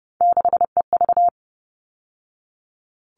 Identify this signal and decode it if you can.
Morse code, heard as a keyed tone, about 30 wpm, 700 Hz. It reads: 6EV
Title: All Cut Numbers - 3-Digit 30wpm